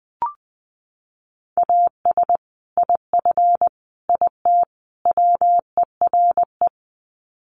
Morse code, heard as a keyed tone, about 20 wpm, 700 Hz. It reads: AS IF IT WERE
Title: Sets of 4 Words 20wpm